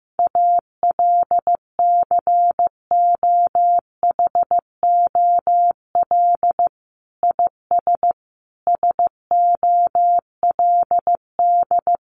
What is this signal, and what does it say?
Morse code, heard as a keyed tone, about 15 wpm, 700 Hz. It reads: ALCOHOL IS SOLD
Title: Sentences from Top 1400 Words 15wpm